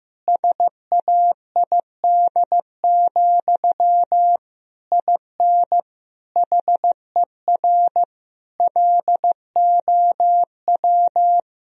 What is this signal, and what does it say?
Morse code, heard as a keyed tone, about 16 wpm, 700 Hz. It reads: SAID, IN HER LOW
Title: Sentences from Top 1000 Words - Review Part 1 - 15wpm